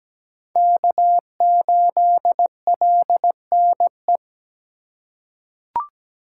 Morse code, heard as a keyed tone, about 17 wpm, 700 Hz. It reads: K8LNE
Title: USA Call Signs - 5 Characters - 2x Character Spacing - 17wpm